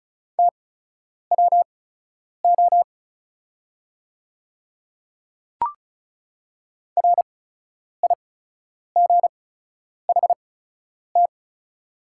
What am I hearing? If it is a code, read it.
Morse code, heard as a keyed tone, about 36 wpm, spaced as for 10 wpm, 700 Hz. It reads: TWO RIGHT
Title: 8x Character Spacing - Top 300 Words - 35wpm